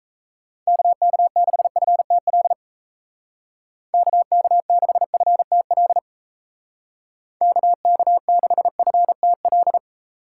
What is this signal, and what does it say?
Morse code, heard as a keyed tone, about 37 wpm, 700 Hz. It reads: KK6FTL KK6FTL KK6FTL